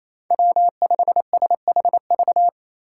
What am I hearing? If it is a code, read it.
Morse code, heard as a keyed tone, about 28 wpm, 690 Hz. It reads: W5SHV
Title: USA Call Signs - 5 Characters - 4x Character Spacing - 28wpm